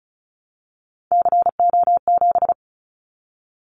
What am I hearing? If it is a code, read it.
Morse code, heard as a keyed tone, about 36 wpm, 700 Hz. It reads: CO7
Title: Callsign Prefixes for Top 100 DXCC - 35wpm